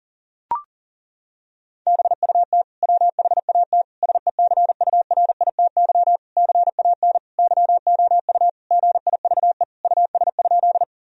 Morse code, heard as a keyed tone, about 37 wpm, 700 Hz. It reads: BUT WHAT SECURITY CAN YOU GIVE US?
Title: Sentences from Top 1000 Words - Review Part 4 - 40wpm